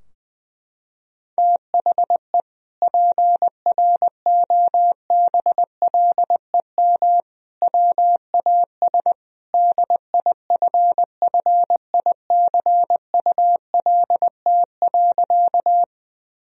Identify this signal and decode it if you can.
Morse code, heard as a keyed tone, about 21 wpm, 700 Hz. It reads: THE PROBLEM WAS DIFFICULT.
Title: Sentences from Top 1000 Words - Review Part 2 - 20wpm